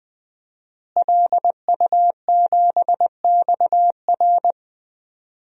Morse code, heard as a keyed tone, about 20 wpm, 700 Hz. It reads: LU7XR
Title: DXCC-100 Call Signs - 5 Characters - Repeated 3x - 20wpm